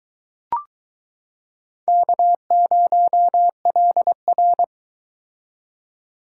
Morse code, heard as a keyed tone, about 23 wpm, 700 Hz. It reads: K0LR